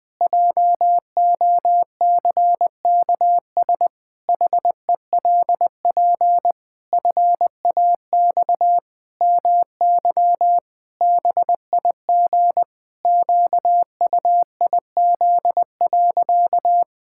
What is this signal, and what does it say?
Morse code, heard as a keyed tone, about 20 wpm, 700 Hz. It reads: JOCKS HELP FAX MY BIG QUIZ.
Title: Pangram Challenge v2 - 20wpm